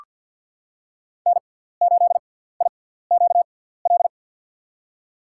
Morse code, heard as a keyed tone, about 48 wpm, spaced as for 17 wpm, 700 Hz. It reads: N8IQL
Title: USA Call Signs - 5 Characters - 6x Character Spacing - 50wpm